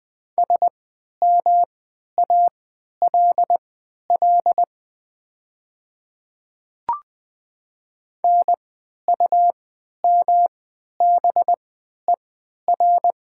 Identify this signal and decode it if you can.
Morse code, heard as a keyed tone, about 20 wpm, spaced as for 11 wpm, 700 Hz. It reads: SMALL NUMBER
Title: 3x Character Spacing - Top 200 Words - 20wpm